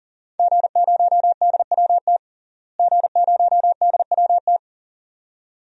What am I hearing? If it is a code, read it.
Morse code, heard as a keyed tone, about 40 wpm, 700 Hz. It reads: G0DWT G0DWT
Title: DXCC-100 Call Signs - 5 Characters - Repeated 2x - 40wpm